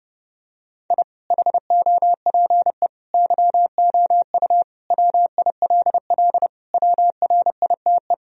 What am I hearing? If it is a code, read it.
Morse code, heard as a keyed tone, about 32 wpm, 700 Hz. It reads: I HOPE YOU WILL WRITE
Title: Sentences from Top 1000 Words - Review Part 1 - 30wpm